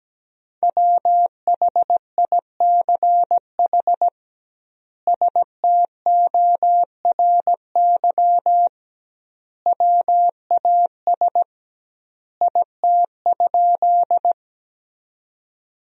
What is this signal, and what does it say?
Morse code, heard as a keyed tone, about 17 wpm, 700 Hz. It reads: WHICH STORY WAS IT?
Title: 2x Word-Spacing - Sentences from Top 300 Words - 17wpm